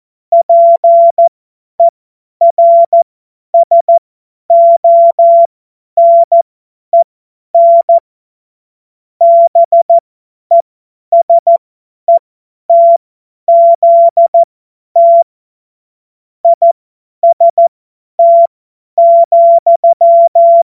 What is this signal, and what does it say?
Morse code, heard as a keyed tone, about 14 wpm, spaced as for 10 wpm, 680 Hz. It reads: PERSONEN BESETZT IST,
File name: Morsekurs / Witze / Witz 26. April 2022 50 F70